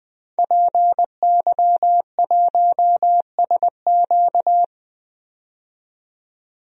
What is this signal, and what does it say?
Morse code, heard as a keyed tone, about 21 wpm, 700 Hz. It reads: PY1SQ
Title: DXCC-100 Call Signs - 5 Characters - 20wpm